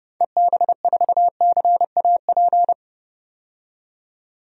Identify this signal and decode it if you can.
Morse code, heard as a keyed tone, about 30 wpm, 690 Hz. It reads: EB4CAP
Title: DXCC-100 Call Signs - 6 Characters - 30wpm